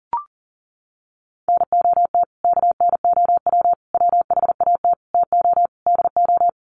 Morse code, heard as a keyed tone, about 40 wpm, 700 Hz. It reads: NOT KNOW WHAT TO DO
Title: Sets of 5 Words 40wpm